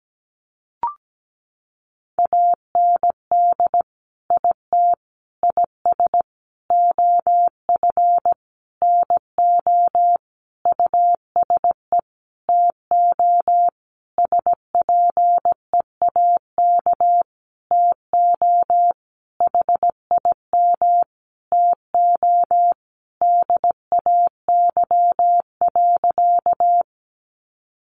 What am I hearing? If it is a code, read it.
Morse code, heard as a keyed tone, about 17 wpm, 700 Hz. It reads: AND IT IS OF NO USE TO SPEAK TO HIM TO DAY.
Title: Sentences from Top 1100 Words 17wpm